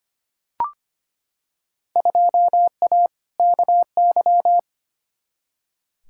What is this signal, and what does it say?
Morse code, heard as a keyed tone, about 25 wpm, 700 Hz. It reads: 2A KY
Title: Field Day Exchanges - 25wpm